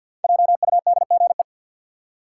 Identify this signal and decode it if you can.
Morse code, heard as a keyed tone, about 47 wpm, 700 Hz. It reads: JUDGE
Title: Five Letter Words - 50wpm